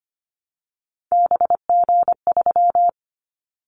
Morse code, heard as a keyed tone, about 25 wpm, 700 Hz. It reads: BG3
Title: Callsign Prefixes for Top 100 DXCC - 25wpm